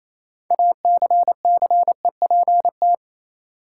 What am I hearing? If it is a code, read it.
Morse code, heard as a keyed tone, about 27 wpm, 700 Hz. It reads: ACCEPT
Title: Sentences from Top 1300 Words - New Word Drill Alt - 28wpm